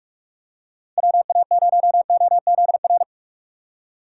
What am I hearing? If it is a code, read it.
Morse code, heard as a keyed tone, about 43 wpm, 700 Hz. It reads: WA0OZR
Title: USA Call Signs - 6 Characters - 6x Character Spacing - 45wpm